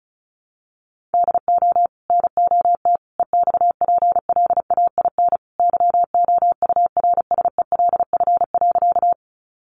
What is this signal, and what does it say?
Morse code, heard as a keyed tone, about 34 wpm, 700 Hz. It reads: DO NOT EXPLAIN YOURSELF.
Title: Sentences from Top 1100 Words 35wpm